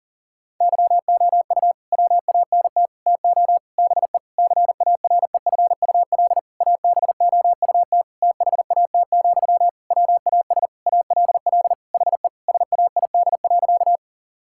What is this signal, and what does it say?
Morse code, heard as a keyed tone, about 40 wpm, 700 Hz. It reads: YOU WANT TO BE CAREFUL ABOUT THAT, WAS ALL HE SAID.
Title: Sentences from Top 1600 Words 40wpm